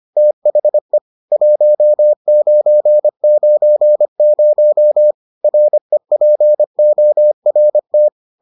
Morse code, heard as a keyed tone, about 25 wpm, 600 Hz. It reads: THE 1990 REPORT